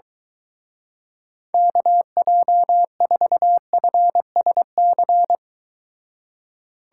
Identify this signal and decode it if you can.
Morse code, heard as a keyed tone, about 23 wpm, 700 Hz. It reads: KJ4FSC